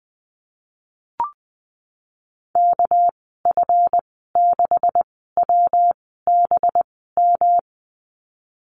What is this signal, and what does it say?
Morse code, heard as a keyed tone, about 20 wpm, spaced as for 14 wpm, 700 Hz. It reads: KF6WBM
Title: 207 Lesson 33g - Letter j - Call Signs 20wpm